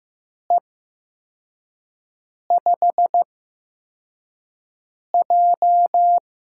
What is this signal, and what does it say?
Morse code, heard as a keyed tone, about 15 wpm, spaced as for 4 wpm, 700 Hz. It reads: E5J
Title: USA Call Signs - 4 Characters - 8x Character Spacing - 15wpm